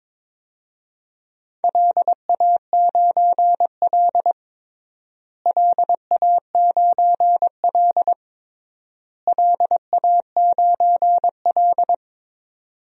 Morse code, heard as a keyed tone, about 22 wpm, 700 Hz. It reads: LA9L LA9L LA9L
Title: DXCC-100 Call Signs - 4 Characters - Repeated 3x - 22wpm